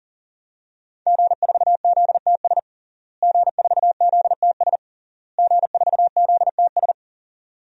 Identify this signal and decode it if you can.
Morse code, heard as a keyed tone, about 40 wpm, 700 Hz. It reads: G4ZTS G4ZTS G4ZTS
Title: DXCC-100 Call Signs - 5 Characters - Repeated 3x - 40wpm